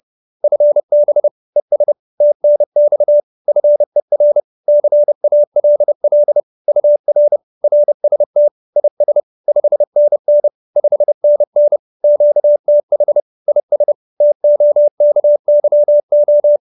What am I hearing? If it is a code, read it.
Morse code, heard as a keyed tone, about 30 wpm, 590 Hz. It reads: FB ES TNX FER CALL UR RST IS 5NN 5NN QTH IS TOKYO
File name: QSO309 30